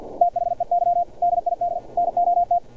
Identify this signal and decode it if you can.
Morse code, heard as a keyed tone, about 50 wpm, 700 Hz. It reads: THEY DID NOT
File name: archived / CRSnet